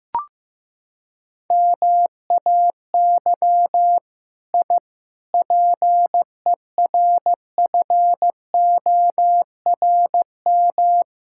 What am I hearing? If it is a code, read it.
Morse code, heard as a keyed tone, about 15 wpm, 700 Hz. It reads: MAY I PERFORM
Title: Sentences from Top 1000 Words - Review Part 5 - 15wpm